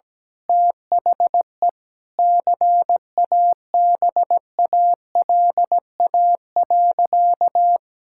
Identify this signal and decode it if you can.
Morse code, heard as a keyed tone, about 17 wpm, 700 Hz. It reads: THE CABALA.
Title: Pangram Challenge v2 - 17wpm